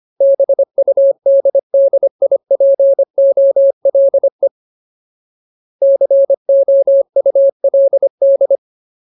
Morse code, heard as a keyed tone, about 25 wpm, 550 Hz. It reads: BUDDIPOLE COULD
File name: Ultimate Word List / 25WPM Ultimate